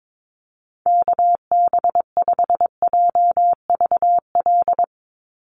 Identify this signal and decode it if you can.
Morse code, heard as a keyed tone, about 22 wpm, 700 Hz. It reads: KB5JVL
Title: USA Call Signs 22wpm